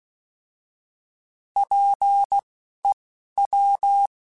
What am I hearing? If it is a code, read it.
Morse code, heard as a keyed tone, about 16 wpm, spaced as for 11 wpm, 800 Hz. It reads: PEW
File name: Alphabetical / Alphabetical mp3s / Ps @10